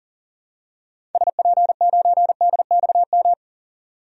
Morse code, heard as a keyed tone, about 40 wpm, 700 Hz. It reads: SP9DXM
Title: DXCC-100 Call Signs - 6 Characters - 40wpm